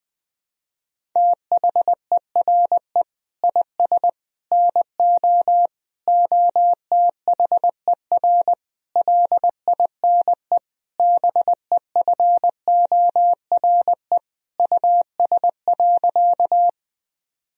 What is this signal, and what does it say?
Morse code, heard as a keyed tone, about 20 wpm, 700 Hz. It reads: THERE IS NO OTHER LINE BEFORE US.
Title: Sentences from Top 1000 Words - Review Part 1 - 20wpm